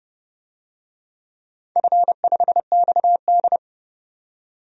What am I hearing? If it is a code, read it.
Morse code, heard as a keyed tone, about 30 wpm, 700 Hz. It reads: F5XD
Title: DXCC-100 Call Signs - 4 Characters - 30wpm